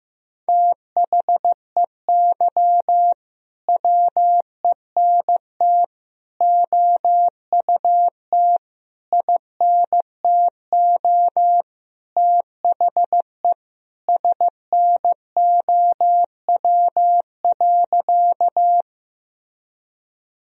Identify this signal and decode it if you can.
Morse code, heard as a keyed tone, about 15 wpm, 700 Hz. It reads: THEY WENT OUT INTO THE SNOW.
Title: Sentences from Top 1000 Words - Review Part 3 - 15wpm